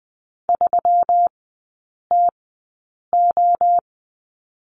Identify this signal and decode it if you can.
Morse code, heard as a keyed tone, about 20 wpm, spaced as for 8 wpm, 700 Hz. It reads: 3TO
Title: 184 Lesson 30e - Letter b - Three Characters 20wpm